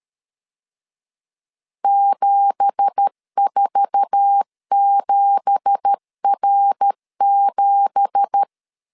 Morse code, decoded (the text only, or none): 747R7